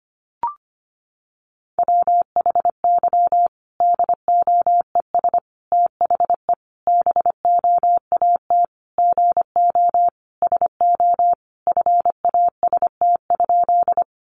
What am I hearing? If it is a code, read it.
Morse code, heard as a keyed tone, about 25 wpm, 700 Hz. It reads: WHY DOES THE BOAT GO SO FAST?
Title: Sentences from Top 500 Words 25wpm